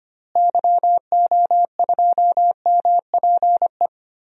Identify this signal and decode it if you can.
Morse code, heard as a keyed tone, about 25 wpm, 700 Hz. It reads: YO2MPE
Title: DXCC-100 Call Signs - 6 Characters - Repeated 2x - 25wpm